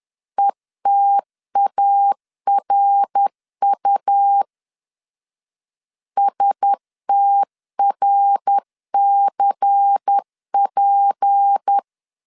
Morse code, heard as a keyed tone, about 10 wpm, 800 Hz. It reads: ETARU STRCP
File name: LEZIONI / 6 PKX0 / 6 PK 10wpm 2436